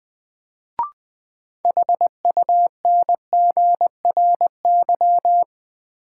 Morse code, heard as a keyed tone, about 20 wpm, 700 Hz. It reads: HUNGRY
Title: Sentences from Top 1400 Words - New Word Drill Alt - 20wpm